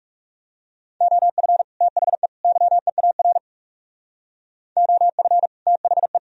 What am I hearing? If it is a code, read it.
Morse code, heard as a keyed tone, about 37 wpm, 700 Hz. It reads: OF THE YEAR OF THE
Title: Speed-Racing - Sets of 3 Words 30wpm